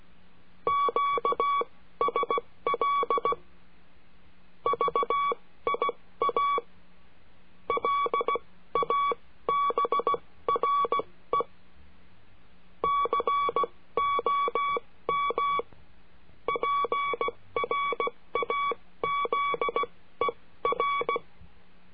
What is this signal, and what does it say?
Morse code, heard as a keyed tone, about 16 wpm, spaced as for 14 wpm, 1130 Hz. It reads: QSL VIA LABRE COM PRAZER
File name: cursos / CURSO DE TELEGRAFIA / Telegrafia - Trilha 22